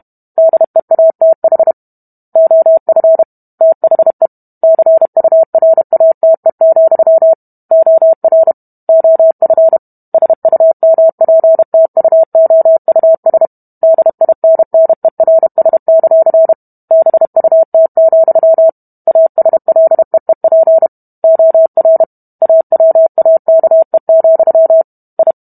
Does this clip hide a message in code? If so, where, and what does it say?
Yes: Morse code from 0.4 to 25.3 s, heard as a keyed tone, about 32 wpm, 650 Hz. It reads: DEATH OF THE CURATE, OR OF SUMPTUOUS DINNERS; BUT, ASLEEP OR AWAKE, I